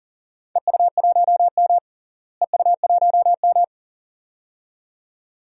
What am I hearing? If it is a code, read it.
Morse code, heard as a keyed tone, about 40 wpm, 700 Hz. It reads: EU1M EU1M
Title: DXCC-100 Call Signs - 4 Characters - Repeated 2x - 40wpm